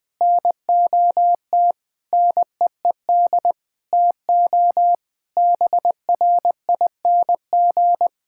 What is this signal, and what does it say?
Morse code, heard as a keyed tone, about 20 wpm, 700 Hz. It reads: NOT NEED TO BRING
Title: Sentences with 4-Word Phrases - Spoken Beginning - 20wpm